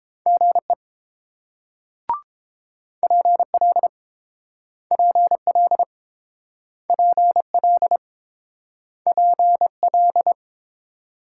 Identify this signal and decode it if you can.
Morse code, heard as a keyed tone, about 30 wpm, 690 Hz. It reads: GE PL PL PL PL